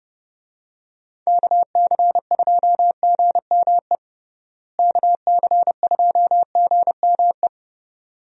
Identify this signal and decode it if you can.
Morse code, heard as a keyed tone, about 30 wpm, 700 Hz. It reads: KC2GME KC2GME
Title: USA Call Signs - 6 Characters - Repeated 2x - 30wpm